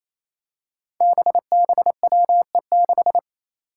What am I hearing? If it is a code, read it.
Morse code, heard as a keyed tone, about 27 wpm, 700 Hz. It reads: BBWE6
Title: Five Challenging Letter-Number Combinations 28wpm